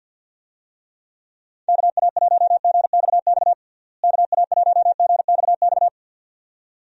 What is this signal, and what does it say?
Morse code, heard as a keyed tone, about 50 wpm, 700 Hz. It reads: KA1GXX KA1GXX
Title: USA Call Signs - 6 Characters - Repeated 2x - 50wpm